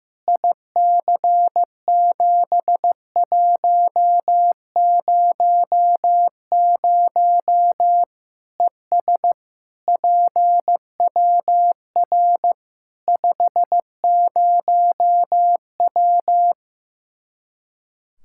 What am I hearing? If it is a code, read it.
Morse code, heard as a keyed tone, about 15 wpm, 700 Hz. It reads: IC7100 ES PWR 50W
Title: Practice QSO - Exchange 2 - Info - 15wpm